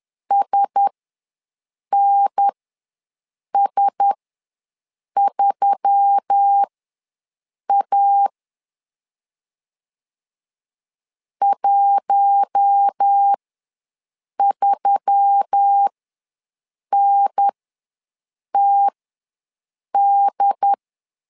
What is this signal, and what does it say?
Morse code, heard as a keyed tone, about 10 wpm, spaced as for 6 wpm, 800 Hz. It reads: SNS3A 13NTD
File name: LEZIONI / 4 Q1S3 ws 10wpm 2437